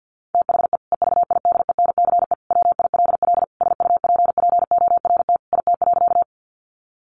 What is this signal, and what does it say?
Morse code, heard as a keyed tone, about 50 wpm, 700 Hz. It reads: THE EVIDENCE WILL SUPPORT IT.
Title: Sentences from Top 1300 Words 50wpm